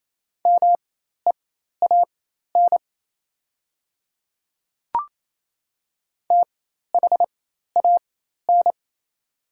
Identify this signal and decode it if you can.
Morse code, heard as a keyed tone, about 28 wpm, spaced as for 13 wpm, 690 Hz. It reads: MEAN THAN
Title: 4x Character Spacing - Top 300 Words - 28wpm